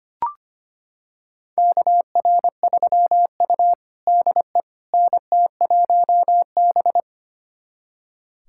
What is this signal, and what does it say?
Morse code, heard as a keyed tone, about 25 wpm, 700 Hz. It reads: KR3U DE NT1B
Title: USA Callsign DE Callsign - 4 Characters - 25wpm